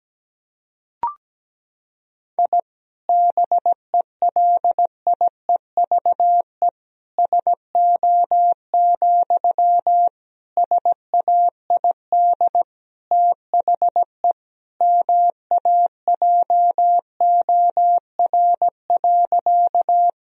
Morse code, heard as a keyed tone, about 17 wpm, 700 Hz. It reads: I BELIEVE SO, SAID THE MAJOR.
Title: Sentences from Top 1000 Words - Review Part 4 - 17wpm